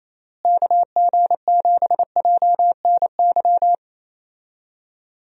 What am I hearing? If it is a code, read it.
Morse code, heard as a keyed tone, about 28 wpm, 700 Hz. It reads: KG7JNY
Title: USA Call Signs - 6 Characters - Repeated 2x - 28wpm